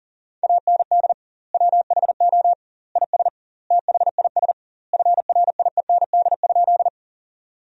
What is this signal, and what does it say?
Morse code, heard as a keyed tone, about 40 wpm, 700 Hz. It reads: AND WHO IS THIS FRIEND?
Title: 2x Word-Spacing - Sentences from Top 300 Words - 40wpm